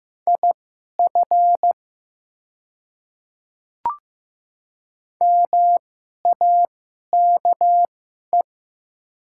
Morse code, heard as a keyed tone, about 15 wpm, spaced as for 11 wpm, 700 Hz. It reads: IF MAKE
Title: 2x Character Spacing - Top 100 Words - 15wpm